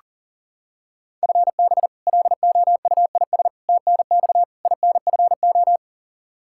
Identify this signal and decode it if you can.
Morse code, heard as a keyed tone, about 40 wpm, 700 Hz. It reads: FB LOUIS TNX INFO
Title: Practice QSO - Exchange 3 - More Info - 40wpm